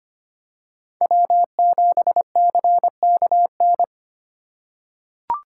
Morse code, heard as a keyed tone, about 25 wpm, 700 Hz. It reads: W7CKN
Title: USA Call Signs - 3x Character Spacing - 25wpm